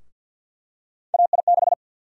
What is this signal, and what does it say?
Morse code, heard as a keyed tone, about 52 wpm, 700 Hz. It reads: AI6
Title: USA Call Sign Prefixes - Repeated 2x - 50wpm